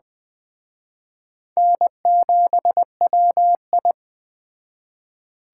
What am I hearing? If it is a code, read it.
Morse code, heard as a keyed tone, about 21 wpm, 700 Hz. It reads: N7WI
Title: Speed Racing - USA 1x2 Call Signs 17wpm